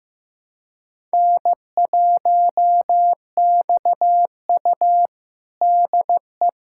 Morse code, heard as a keyed tone, about 15 wpm, 700 Hz. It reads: N1XU DE
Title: USA Callsign DE Callsign - 4 Characters - 15wpm